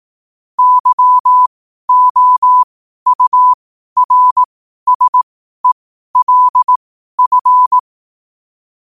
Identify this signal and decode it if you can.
Morse code, heard as a keyed tone, about 18 wpm, spaced as for 12 wpm, 1000 Hz. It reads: YOURSELF
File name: qotdfiles / 12wpm